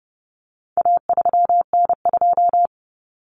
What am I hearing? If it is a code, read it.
Morse code, heard as a keyed tone, about 30 wpm, 700 Hz. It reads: A3N2
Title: Common Cut Numbers - 4-Digit 30wpm